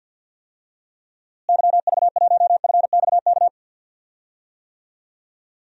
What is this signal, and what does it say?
Morse code, heard as a keyed tone, about 52 wpm, 700 Hz. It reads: YV1FXK
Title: DXCC-100 Call Signs - 6 Characters - Repeated 3x - 50wpm